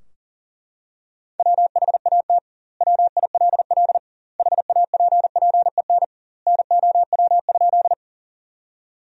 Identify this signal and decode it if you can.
Morse code, heard as a keyed tone, about 37 wpm, 700 Hz. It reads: WHAT WILL HAPPEN NOW?
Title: 2x Word-Spacing - Sentences from Top 400 Words - 40wpm